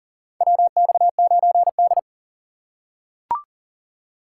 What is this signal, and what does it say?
Morse code, heard as a keyed tone, about 40 wpm, 700 Hz. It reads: WX9D
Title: USA Call Signs - 4 Characters - 2x Character Spacing - 40wpm